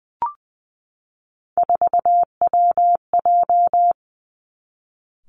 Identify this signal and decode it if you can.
Morse code, heard as a keyed tone, about 20 wpm, 700 Hz. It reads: 4WJ
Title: Three Challenging Letter-Number Combinations 20wpm